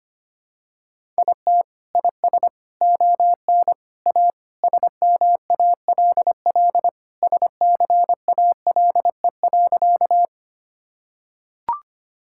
Morse code, heard as a keyed tone, about 25 wpm, 700 Hz. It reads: IT IS ON A SMALL SCALE.
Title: Sentences from Top 1000 Words - Review Part 5 - 25wpm